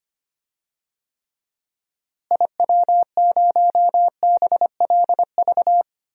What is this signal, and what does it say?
Morse code, heard as a keyed tone, about 25 wpm, 700 Hz. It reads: IW0BLV
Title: DXCC-100 Call Signs - 6 Characters - Repeated 2x - 25wpm